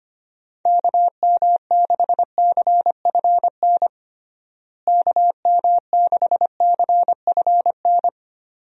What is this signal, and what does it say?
Morse code, heard as a keyed tone, about 25 wpm, 700 Hz. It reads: KM6CFN KM6CFN
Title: USA Call Signs - 6 Characters - Repeated 2x - 25wpm